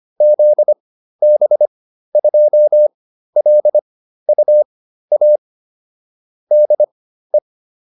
Morse code, heard as a keyed tone, about 25 wpm, spaced as for 13 wpm, 600 Hz. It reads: ZB2LUA DE
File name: QSO402 13